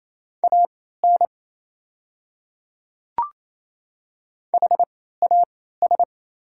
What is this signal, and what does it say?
Morse code, heard as a keyed tone, about 28 wpm, spaced as for 16 wpm, 690 Hz. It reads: AN HAS
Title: 3x Character Spacing - Top 200 Words - 28wpm